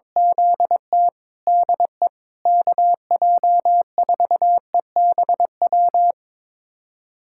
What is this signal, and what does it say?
Morse code, heard as a keyed tone, about 21 wpm, 700 Hz. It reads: ZT DE KJ4EBW
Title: USA Callsign DE Callsign 22wpm